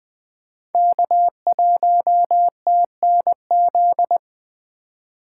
Morse code, heard as a keyed tone, about 20 wpm, 700 Hz. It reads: K1TNZ
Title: USA Call Signs - 5 Characters - 4x Character Spacing - 20wpm